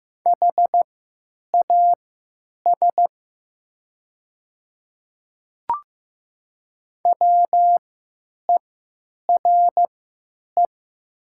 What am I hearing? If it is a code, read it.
Morse code, heard as a keyed tone, about 15 wpm, spaced as for 9 wpm, 700 Hz. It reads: HAS WERE
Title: 3x Character Spacing - Top 100 Words - 15wpm